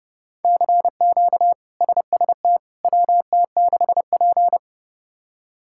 Morse code, heard as a keyed tone, about 30 wpm, 700 Hz. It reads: CQ SST WT6P
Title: K1USN SST QSO Practice - 30wpm